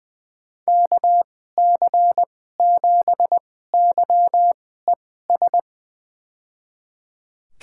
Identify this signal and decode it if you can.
Morse code, heard as a keyed tone, about 20 wpm, spaced as for 14 wpm, 700 Hz. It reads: KC7YES
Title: USA Call Signs - 2x Character Spacing - 20wpm